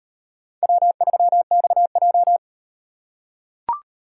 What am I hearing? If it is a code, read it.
Morse code, heard as a keyed tone, about 38 wpm, 700 Hz. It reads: W3XJ